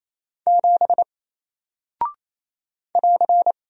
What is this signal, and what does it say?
Morse code, heard as a keyed tone, about 27 wpm, 700 Hz. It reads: 7 <AR>
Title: Single Character (Letter:Number:Punctuation:Prosign) 28wpm